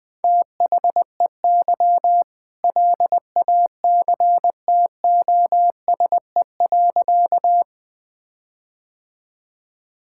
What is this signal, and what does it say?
Morse code, heard as a keyed tone, about 21 wpm, 700 Hz. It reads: THEY LACTOSE.
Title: All-Age Appropriate Jokes - 20wpm